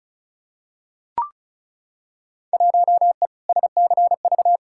Morse code, heard as a keyed tone, about 34 wpm, 700 Hz. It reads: 1E SCV
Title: Field Day Exchanges - 35wpm